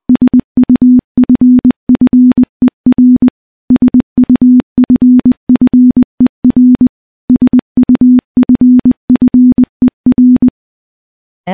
Morse code, heard as a keyed tone, about 20 wpm, 250 Hz. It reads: SUFFER SUFFER SUFFER
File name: morse-training-20WPM